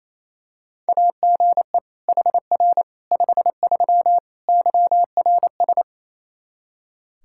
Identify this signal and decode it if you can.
Morse code, heard as a keyed tone, about 28 wpm, 700 Hz. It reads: AGE HR 53 YRS
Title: Practice QSO - Exchange 3 - More Info - 28wpm